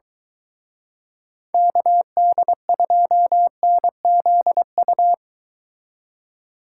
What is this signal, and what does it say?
Morse code, heard as a keyed tone, about 23 wpm, 700 Hz. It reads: KD2NZU